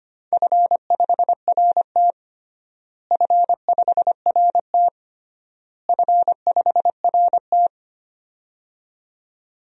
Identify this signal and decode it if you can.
Morse code, heard as a keyed tone, about 25 wpm, 700 Hz. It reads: F5RT F5RT F5RT
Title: DXCC-100 Call Signs - 4 Characters - Repeated 3x - 25wpm